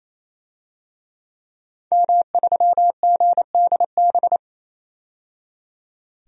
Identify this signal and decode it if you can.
Morse code, heard as a keyed tone, about 27 wpm, 700 Hz. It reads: M3GDB
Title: DXCC-100 Call Signs - 5 Characters - 28wpm